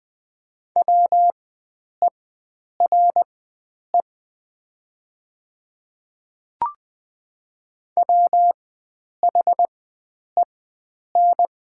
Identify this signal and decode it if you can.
Morse code, heard as a keyed tone, about 20 wpm, spaced as for 9 wpm, 700 Hz. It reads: WERE WHEN
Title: 4x Character Spacing - Top 500 Words - 20wpm